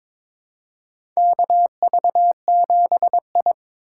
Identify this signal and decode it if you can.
Morse code, heard as a keyed tone, about 22 wpm, 700 Hz. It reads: KV7I
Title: USA Call Signs - 3x Character Spacing - 22wpm